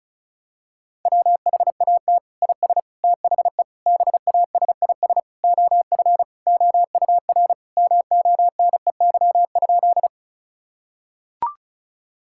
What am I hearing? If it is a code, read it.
Morse code, heard as a keyed tone, about 34 wpm, 700 Hz. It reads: WHAT IS THE BASIS OF OUR MONEY?
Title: Sentences from Top 1000 Words - Review Part 5 - 35wpm